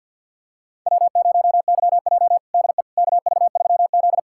Morse code, heard as a keyed tone, about 46 wpm, 690 Hz. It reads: W0YJ DE KV3Z
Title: USA Callsign DE Callsign - 4 Characters - 50wpm